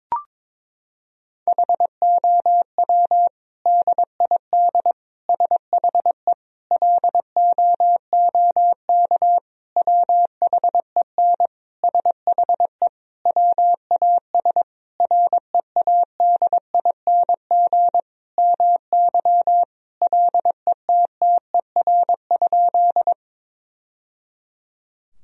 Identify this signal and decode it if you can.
Morse code, heard as a keyed tone, about 22 wpm, 700 Hz. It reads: HOW DID SHE LOOK WHEN SHE WAS READING MY LETTER?
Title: Sentences from Top 1000 Words - Review Part 3 - 22wpm